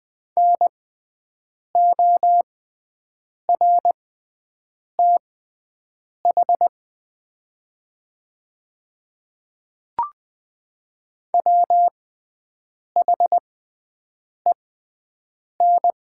Morse code, heard as a keyed tone, about 20 wpm, spaced as for 7 wpm, 700 Hz. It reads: NORTH WHEN
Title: 6x Character Spacing - Top 300 Words - 20wpm